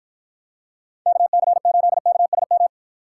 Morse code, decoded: KC8KSM

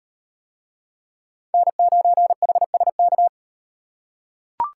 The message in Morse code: N9HSK

